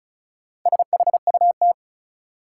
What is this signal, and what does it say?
Morse code, heard as a keyed tone, about 35 wpm, 700 Hz. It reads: SHUT
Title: Four Letter Words - 35wpm